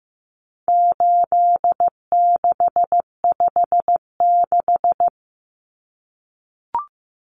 Morse code, heard as a keyed tone, about 15 wpm, 700 Hz. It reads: 8656
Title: Common Cut Numbers - 4-Digit 15wpm